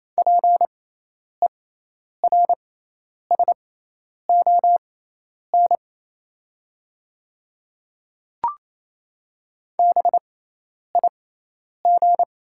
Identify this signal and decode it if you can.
Morse code, heard as a keyed tone, about 28 wpm, spaced as for 10 wpm, 700 Hz. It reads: PERSON BIG
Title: 6x Character Spacing - Top 500 Words - 28wpm